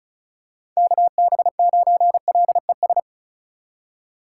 Morse code, heard as a keyed tone, about 36 wpm, 690 Hz. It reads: KB9LES
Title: USA Call Signs - 6 Characters - 4x Character Spacing - 35wpm